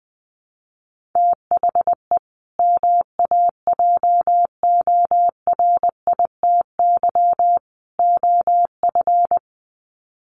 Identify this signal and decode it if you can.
Morse code, heard as a keyed tone, about 20 wpm, 700 Hz. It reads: THE MAJORITY OF
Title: Sets of 3 Words 20wpm